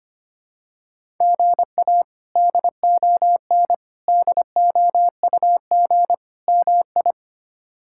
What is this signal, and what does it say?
Morse code, heard as a keyed tone, about 25 wpm, 700 Hz. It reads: GA DON DOUG MI
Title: K1USN SST QSO Practice - 25wpm